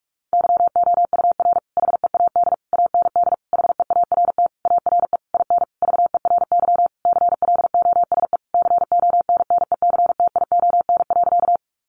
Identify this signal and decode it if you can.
Morse code, heard as a keyed tone, about 44 wpm, 700 Hz. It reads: YOUR HEAD AND HEART ARE IN VERY CLOSE CONNECTION.
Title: Sentences from Top 1300 Words 45wpm